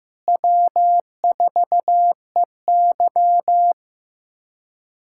Morse code, heard as a keyed tone, about 15 wpm, 700 Hz. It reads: W4EY
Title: USA Callsign DE Callsign - 4 Characters - 15wpm